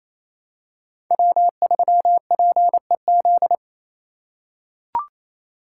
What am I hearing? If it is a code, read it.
Morse code, heard as a keyed tone, about 28 wpm, 700 Hz. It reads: W3PEZ
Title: USA Call Signs - 3x Character Spacing - 28wpm